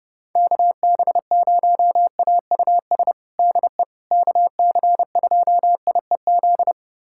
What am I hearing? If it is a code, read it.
Morse code, heard as a keyed tone, about 32 wpm, 700 Hz. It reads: KB0AUS DE KC2IEZ
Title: USA Callsign DE Callsign 30wpm